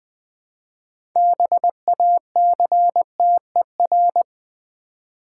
Morse code, heard as a keyed tone, about 20 wpm, 700 Hz. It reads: BACTER
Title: Build A Word - 2 Letters At A Time - 8-Letter Words - 20wpm